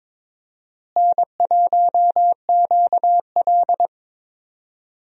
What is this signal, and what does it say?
Morse code, heard as a keyed tone, about 22 wpm, 700 Hz. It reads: N1QL
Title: USA Call Signs - 4 Characters - Repeated 2x - 22wpm